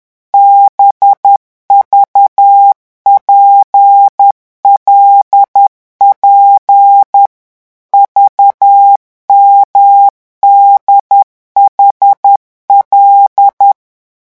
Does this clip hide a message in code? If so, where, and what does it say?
Yes: Morse code from 0.3 to 13.7 s, heard as a keyed tone, about 10 wpm, 800 Hz. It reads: BVPLP VMDHL